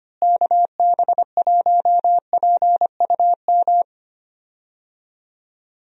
Morse code, heard as a keyed tone, about 25 wpm, 700 Hz. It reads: KB1PUM
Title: USA Call Signs - 6 Characters - Repeated 2x - 25wpm